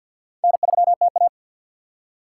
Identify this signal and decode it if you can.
Morse code, heard as a keyed tone, about 65 wpm, spaced as for 57 wpm, 700 Hz. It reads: N3TA
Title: Speed Racing - USA 1x2 Call Signs 50wpm